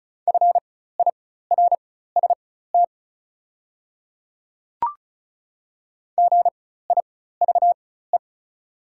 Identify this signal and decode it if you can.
Morse code, heard as a keyed tone, about 36 wpm, spaced as for 16 wpm, 700 Hz. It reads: FIRST GIVE
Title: 4x Character Spacing - Top 200 Words - 35wpm